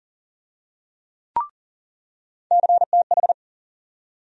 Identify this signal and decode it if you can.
Morse code, heard as a keyed tone, about 40 wpm, 700 Hz. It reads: CTH
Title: USA Call Sign Postfixes - 3 Letters 40wpm